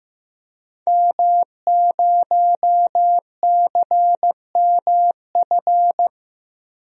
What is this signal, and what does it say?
Morse code, heard as a keyed tone, about 15 wpm, 700 Hz. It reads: M0CMF
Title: DXCC-100 Call Signs - 5 Characters - Repeated 2x - 15wpm